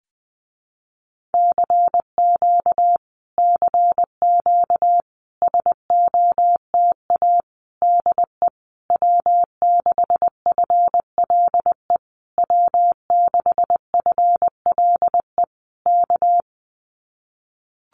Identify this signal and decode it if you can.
Morse code, heard as a keyed tone, about 20 wpm, 700 Hz. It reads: CQ CQ SOTA DE W6FLE W6FLE K
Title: Common QSO Phrases Encore 20wpm